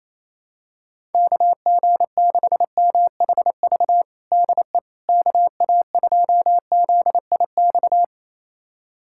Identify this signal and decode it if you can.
Morse code, heard as a keyed tone, about 28 wpm, 700 Hz. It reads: KG6MHV DE KA2ZIX
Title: USA Callsign DE Callsign 28wpm